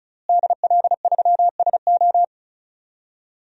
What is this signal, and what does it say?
Morse code, heard as a keyed tone, about 36 wpm, 700 Hz. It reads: DL3SO
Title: DXCC-100 Call Signs - 5 Characters - Repeated 3x - 35wpm